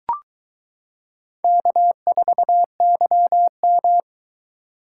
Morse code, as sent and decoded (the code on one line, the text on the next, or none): -.- ....- -.-- --
K4YM